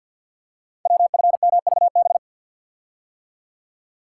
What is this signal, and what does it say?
Morse code, heard as a keyed tone, about 49 wpm, 700 Hz. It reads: WFMVB
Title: Code Groups - 5 Letters - 50wpm